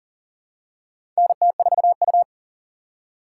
Morse code, heard as a keyed tone, about 40 wpm, 700 Hz. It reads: NT4U
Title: USA Call Signs - 6x Character Spacing - 40wpm